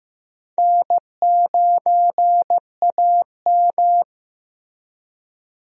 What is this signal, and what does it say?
Morse code, heard as a keyed tone, about 15 wpm, 700 Hz. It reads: N9AM
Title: USA Call Signs - 4 Characters - Repeated 2x - 15wpm